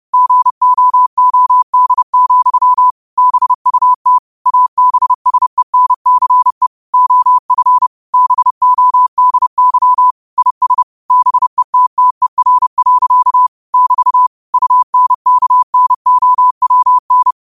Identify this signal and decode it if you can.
Morse code, heard as a keyed tone, about 30 wpm, 1000 Hz. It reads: GOOD, BUT ABSENCE OF BODY IS BETTER. <BT> UNKNOWN